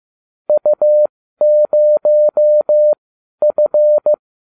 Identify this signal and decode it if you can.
Morse code, heard as a keyed tone, about 15 wpm, spaced as for 12 wpm, 600 Hz. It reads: U0F